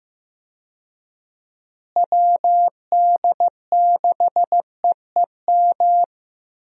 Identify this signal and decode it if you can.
Morse code, heard as a keyed tone, about 15 wpm, 700 Hz. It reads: WD6EEM
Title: USA Call Signs - 6 Characters - Repeated 2x - 15wpm